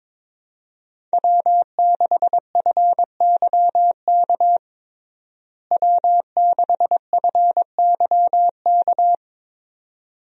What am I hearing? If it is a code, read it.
Morse code, heard as a keyed tone, about 22 wpm, 700 Hz. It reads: W6FYK W6FYK
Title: USA Call Signs - 5 Characters - Repeated 2x - 22wpm